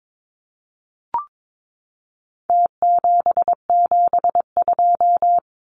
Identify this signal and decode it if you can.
Morse code, heard as a keyed tone, about 22 wpm, 700 Hz. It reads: T772
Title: Numbers - Contest Style - 4-Digit 22wpm